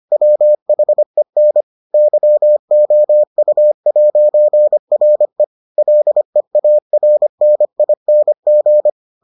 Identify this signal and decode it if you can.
Morse code, heard as a keyed tone, about 25 wpm, 600 Hz. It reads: WHEN YOU'RE LEARNING